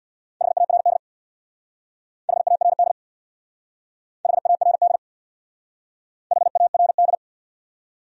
Morse code, heard as a keyed tone, about 64 wpm, 700 Hz. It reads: HARD HARD HARD HARD